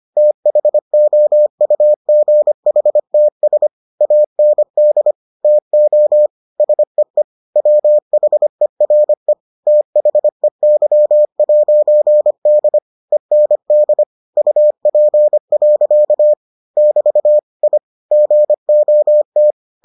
Morse code, heard as a keyed tone, about 25 wpm, 600 Hz. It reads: THOUGHTS AND TO SEE WHERE THEY'D END UP. <BT> I GOT